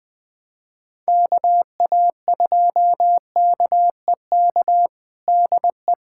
Morse code, heard as a keyed tone, about 21 wpm, 700 Hz. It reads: KA2KEK DE
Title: USA Callsign DE Callsign - 6 Characters - 20wpm